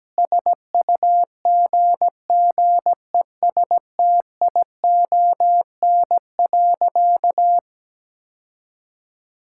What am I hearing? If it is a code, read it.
Morse code, heard as a keyed tone, about 17 wpm, 700 Hz. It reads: SUGGESTION.
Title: Sentences from Top 1600 Words 17wpm